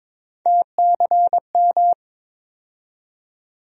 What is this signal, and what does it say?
Morse code, heard as a keyed tone, about 22 wpm, 700 Hz. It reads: TCM
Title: USA Call Sign Postfixes - 3 Letters 22wpm